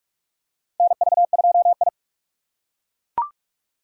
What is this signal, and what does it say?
Morse code, heard as a keyed tone, about 42 wpm, 700 Hz. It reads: NV2I
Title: USA Call Signs - 4 Characters - 6x Character Spacing - 45wpm